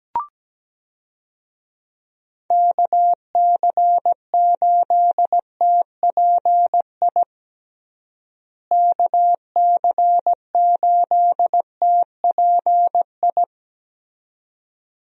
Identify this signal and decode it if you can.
Morse code, heard as a keyed tone, about 17 wpm, 700 Hz. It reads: KC8TPI KC8TPI
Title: USA Call Signs - 6 Characters - Repeated 2x - 17wpm